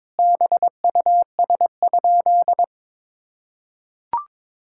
Morse code, heard as a keyed tone, about 22 wpm, 700 Hz. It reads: BUS?
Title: Sentences from Top 1000 Words - Review Part 6 - 22wpm